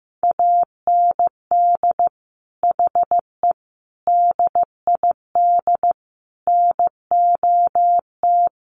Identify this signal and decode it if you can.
Morse code, heard as a keyed tone, about 15 wpm, 700 Hz. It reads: AND HE DID NOT
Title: Sentences from Top 1300 Words 15wpm